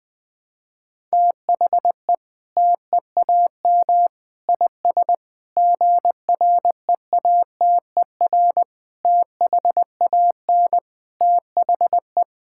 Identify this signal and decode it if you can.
Morse code, heard as a keyed tone, about 20 wpm, 700 Hz. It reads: THE TEAM IS GREATER THAN THE
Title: Sentences with 8-Word Phrases - 20wpm